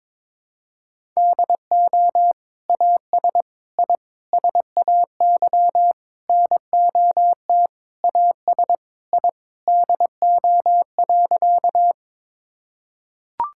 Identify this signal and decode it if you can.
Morse code, heard as a keyed tone, about 22 wpm, 700 Hz. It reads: DO AS I SAY NOT AS I DO.
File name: practice / 22wpm / Sentences with 8-Word Phrases - 22wpm